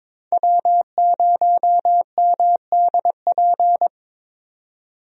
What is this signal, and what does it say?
Morse code, heard as a keyed tone, about 21 wpm, 700 Hz. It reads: W0MDP
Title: USA Call Signs - 5 Characters - 22wpm